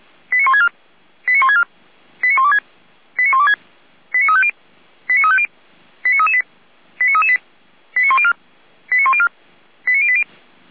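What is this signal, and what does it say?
Five-tone selective call (ZVEI), heard as quick runs of five tones, tones of about 70 ms. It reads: 89164 89164 89126 89126 89350 89350 89338 89338 89104 89104 89080
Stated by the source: ZVEI five-tone selective call received on an SDR.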